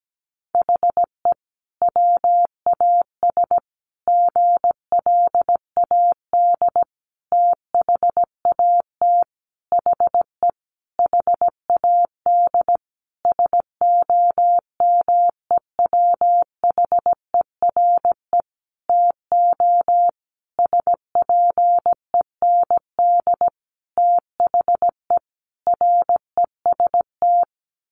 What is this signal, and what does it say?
Morse code, heard as a keyed tone, about 17 wpm, 700 Hz. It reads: HE WAS GLAD THAT HE HAD SOMEWHERE TO SPEND THE REST
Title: Sentences from Top 1400 Words 17wpm